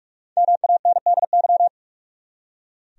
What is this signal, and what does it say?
Morse code, heard as a keyed tone, about 46 wpm, 700 Hz. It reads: MANDY
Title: USA Common Names - 45wpm